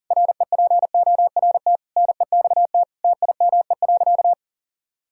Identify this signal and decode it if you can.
Morse code, heard as a keyed tone, about 40 wpm, 700 Hz. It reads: REPORT NEXT TIME.
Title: Sentences from Top 1000 Words - Review Part 6 - 40wpm